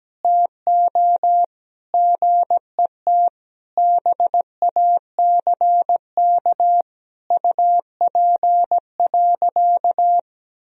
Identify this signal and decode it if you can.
Morse code, heard as a keyed tone, about 17 wpm, 700 Hz. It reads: TO GET BACK UP.
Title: Sentences from Top 1000 Words - Review Part 2 - 17wpm